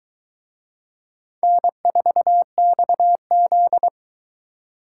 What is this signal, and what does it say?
Morse code, heard as a keyed tone, about 22 wpm, 700 Hz. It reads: N4XZ